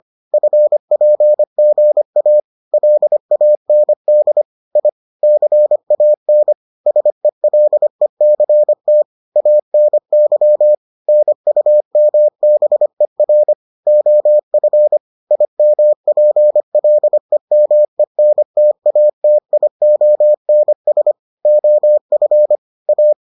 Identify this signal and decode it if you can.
Morse code, heard as a keyed tone, about 25 wpm, 600 Hz. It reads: FPGA LAND I CAN SELECT ANY NUMBER OF IMPLEMENTATIONS OF A